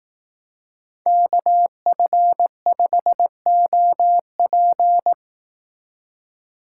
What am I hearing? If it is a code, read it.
Morse code, heard as a keyed tone, about 19 wpm, 700 Hz. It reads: KF5OP
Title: Speed Racing - USA 2x2 Call Signs 15wpm